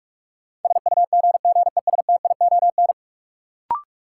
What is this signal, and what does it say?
Morse code, heard as a keyed tone, about 47 wpm, 700 Hz. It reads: SUGGESTION
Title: Sentences from Top 1600 Words - New Word Drill - 45wpm